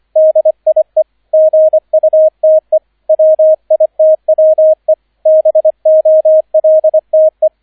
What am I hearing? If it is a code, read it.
Morse code, heard as a keyed tone, about 25 wpm, 630 Hz, with radio noise. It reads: DIE GUTE WITWE BOLTE